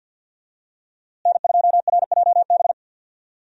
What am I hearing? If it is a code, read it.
Morse code, heard as a keyed tone, about 52 wpm, 700 Hz. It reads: N2RJB
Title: USA Call Signs - 5 Characters - Repeated 2x - 50wpm